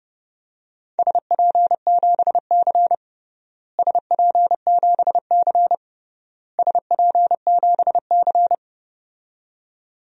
Morse code, heard as a keyed tone, about 32 wpm, 700 Hz. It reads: SP7C SP7C SP7C
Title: DXCC-100 Call Signs - 4 Characters - Repeated 3x - 30wpm